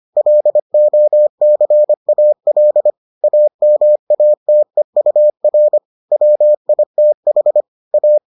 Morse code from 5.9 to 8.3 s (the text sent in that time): WITH A